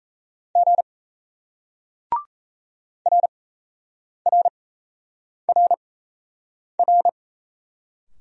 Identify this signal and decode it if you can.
Morse code, heard as a keyed tone, about 38 wpm, spaced as for 8 wpm, 700 Hz. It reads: G RRRR